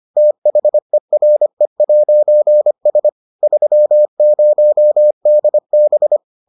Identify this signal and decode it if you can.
Morse code, heard as a keyed tone, about 25 wpm, 600 Hz. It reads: THERE'S 30DB